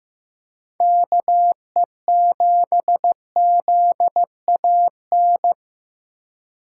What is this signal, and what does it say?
Morse code, heard as a keyed tone, about 16 wpm, 700 Hz. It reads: KE7ZAN
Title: USA Call Signs - 6 Characters - 2x Character Spacing - 15wpm